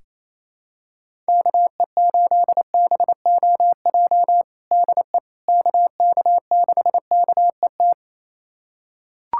Morse code, heard as a keyed tone, about 28 wpm, 700 Hz. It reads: KE8BOJ DE KK6KET
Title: USA Callsign DE Callsign - 6 Characters - 28wpm